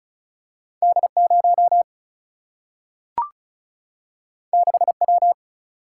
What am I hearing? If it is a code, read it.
Morse code, heard as a keyed tone, about 36 wpm, 700 Hz. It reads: D 0 6 W
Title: Single-Letter Single-Number 35wpm